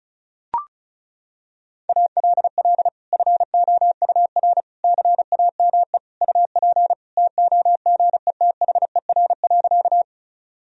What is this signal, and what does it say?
Morse code, heard as a keyed tone, about 36 wpm, 700 Hz. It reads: ALL FOUR CAME UP TOGETHER.
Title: Sentences from Top 1000 Words - Review Part 1 - 35wpm